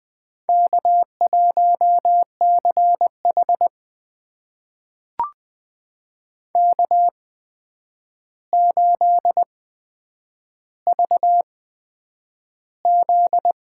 Morse code, heard as a keyed tone, about 20 wpm, spaced as for 9 wpm, 700 Hz. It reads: K1CH K8VZ
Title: USA Call Signs - 4 Characters - 8x Character Spacing - 20wpm